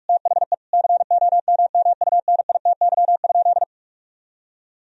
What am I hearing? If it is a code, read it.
Morse code, heard as a keyed tone, about 41 wpm, 700 Hz. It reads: THE COMMUNITY?
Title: Sentences from Top 1000 Words - Review Part 4 - 45wpm